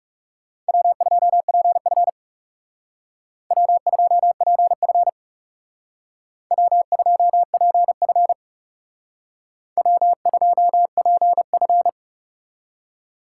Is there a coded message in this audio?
Yes: Morse code from 0.7 to 11.9 s, heard as a keyed tone, about 38 wpm, 700 Hz. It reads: W2PF W2PF W2PF W2PF